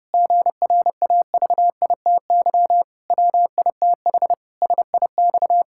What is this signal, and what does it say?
Morse code, heard as a keyed tone, about 30 wpm, 700 Hz. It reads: GRAVITY WITH SIX
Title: Pangram Challenge v2 - 30wpm